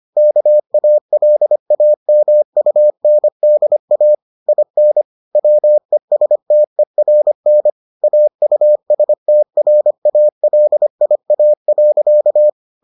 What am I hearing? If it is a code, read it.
Morse code, heard as a keyed tone, about 25 wpm, 600 Hz. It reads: KALAMUNDA IN WESTERN AUSTRALIA.